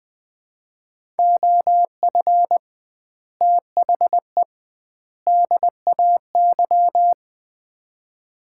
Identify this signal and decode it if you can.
Morse code, heard as a keyed tone, about 20 wpm, 700 Hz. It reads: OF THE DAY
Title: 2x Word-Spacing - Sets of 3 Words - 20wpm